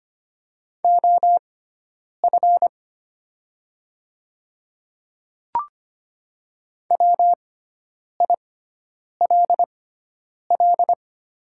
Morse code, heard as a keyed tone, about 25 wpm, spaced as for 9 wpm, 700 Hz. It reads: OF WILL
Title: 6x Character Spacing - Top 100 Words - 25wpm